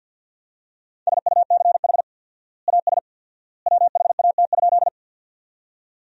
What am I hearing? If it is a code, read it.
Morse code, heard as a keyed tone, about 46 wpm, 700 Hz. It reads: SUCH AS WHAT?
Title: 4x Word-Spacing - Sentences from Top 200 Words - 50wpm